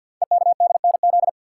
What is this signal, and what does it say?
Morse code, heard as a keyed tone, about 45 wpm, 690 Hz. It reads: EKDNZ
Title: Code Groups - 5 Letters - 50wpm